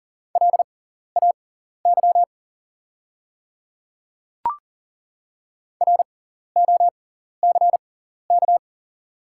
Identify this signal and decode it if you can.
Morse code, heard as a keyed tone, about 40 wpm, spaced as for 14 wpm, 700 Hz. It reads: LAY ROCK
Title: 6x Character Spacing - Top 500 Words - 40wpm